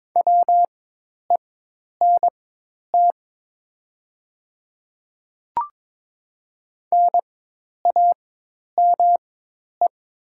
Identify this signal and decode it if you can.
Morse code, heard as a keyed tone, about 22 wpm, spaced as for 10 wpm, 690 Hz. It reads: WENT NAME
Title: 4x Character Spacing - Top 300 Words - 22wpm